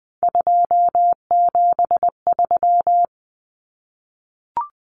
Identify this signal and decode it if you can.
Morse code, heard as a keyed tone, about 20 wpm, 700 Hz. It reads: 273
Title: Common Cut Numbers - 3-Digit 20wpm